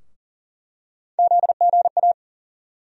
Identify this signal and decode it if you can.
Morse code, heard as a keyed tone, about 40 wpm, 700 Hz. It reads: ZGA
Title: USA Call Sign Postfixes - 3 Letters 40wpm